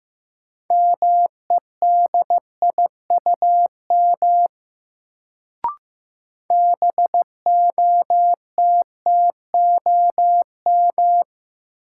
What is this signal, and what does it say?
Morse code, heard as a keyed tone, about 15 wpm, 700 Hz. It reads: MEDIUM BOTTOM
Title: Sentences from Top 900 Words - New Word Drill - 15wpm